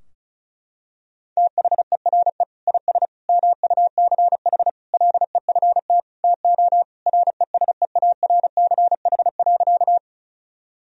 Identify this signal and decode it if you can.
Morse code, heard as a keyed tone, about 36 wpm, 700 Hz. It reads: THERE IS MUCH LEFT TO RESEARCH.
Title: Sentences from Top 1000 Words - Review Part 4 - 35wpm